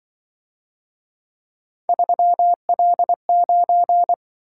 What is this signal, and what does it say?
Morse code, heard as a keyed tone, about 25 wpm, 700 Hz. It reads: 3L9
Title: Speed-Racing - Three Letter-Number Combinations - 20wpm